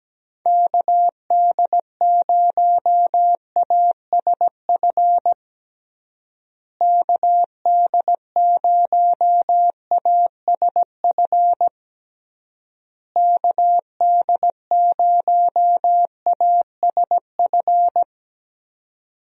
Morse code, heard as a keyed tone, about 17 wpm, 700 Hz. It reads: KD0ASF KD0ASF KD0ASF
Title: USA Call Signs - 6 Characters - Repeated 3x - 17wpm